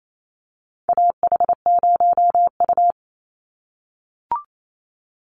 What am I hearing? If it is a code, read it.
Morse code, heard as a keyed tone, about 28 wpm, 700 Hz. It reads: AH0U
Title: Call Signs - 28wpm